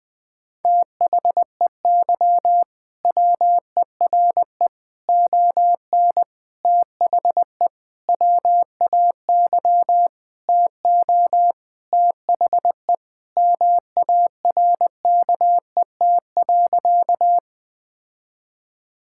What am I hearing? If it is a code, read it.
Morse code, heard as a keyed tone, about 20 wpm, 700 Hz. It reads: THEY WERE ON THE WAY TO THE MARKET.
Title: Sentences with 5-Word Phrases - 20wpm